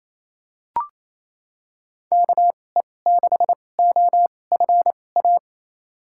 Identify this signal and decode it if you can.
Morse code, heard as a keyed tone, about 28 wpm, spaced as for 20 wpm, 700 Hz. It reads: KE6OFA
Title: USA Call Signs - 6 Characters - 2x Character Spacing - 28wpm